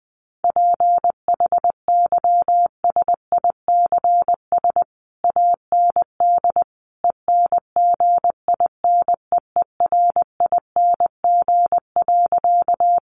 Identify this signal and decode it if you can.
Morse code, heard as a keyed tone, about 20 wpm, 700 Hz. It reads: PHYSICS AND ENGINEERING.
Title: Sentences from Top 1500 Words 20wpm